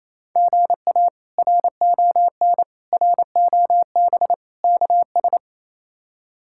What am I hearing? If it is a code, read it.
Morse code, heard as a keyed tone, about 28 wpm, 700 Hz. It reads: GA RON ROB KS
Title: K1USN SST QSO Practice - 28wpm